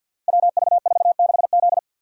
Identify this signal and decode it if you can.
Morse code, heard as a keyed tone, about 48 wpm, 700 Hz. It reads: WV46Z
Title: Code Groups - 5 Letter:Numbers - 50wpm